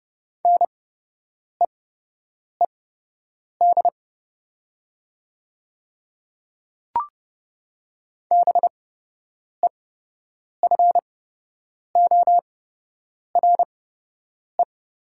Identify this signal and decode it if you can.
Morse code, heard as a keyed tone, about 30 wpm, spaced as for 8 wpm, 700 Hz. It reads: NEED BEFORE
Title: 8x Character Spacing - Top 200 Words - 30wpm